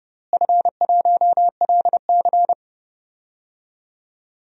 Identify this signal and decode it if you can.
Morse code, heard as a keyed tone, about 29 wpm, 700 Hz. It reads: F1LC
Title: DXCC-100 Call Signs - 4 Characters - Repeated 3x - 30wpm